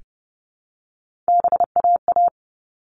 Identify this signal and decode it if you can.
Morse code, heard as a keyed tone, about 30 wpm, 700 Hz. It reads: BAA
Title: All Cut Numbers - 3-Digit 30wpm